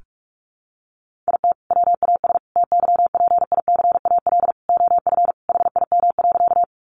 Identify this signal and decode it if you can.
Morse code, heard as a keyed tone, about 44 wpm, 700 Hz. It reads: IT WAS TYPICAL OF HIM.
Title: Sentences from Top 1100 Words 45wpm